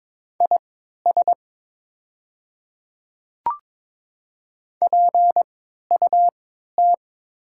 Morse code, heard as a keyed tone, about 22 wpm, spaced as for 12 wpm, 700 Hz. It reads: IS PUT
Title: 3x Character Spacing - Top 300 Words - 22wpm